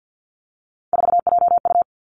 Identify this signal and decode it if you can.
Morse code, heard as a keyed tone, about 50 wpm, 700 Hz. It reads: 4JU
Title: Three Challenging Letter-Number Combinations 50wpm